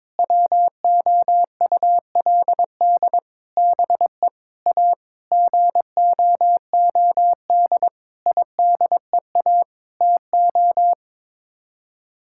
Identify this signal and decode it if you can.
Morse code, heard as a keyed tone, about 22 wpm, 700 Hz. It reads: WOULD BE A GOOD IDEA TO
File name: Sets of 9 Words - 22wpm